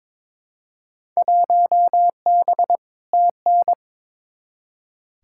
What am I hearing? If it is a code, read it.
Morse code, heard as a keyed tone, about 22 wpm, 700 Hz. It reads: 1B TN
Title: Field Day Exchanges - 22wpm